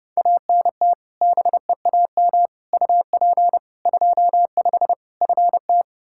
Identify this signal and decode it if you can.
Morse code, heard as a keyed tone, about 32 wpm, 700 Hz. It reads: ANT BEAM UP 25 FT
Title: K1USN SST QSO Practice 30wpm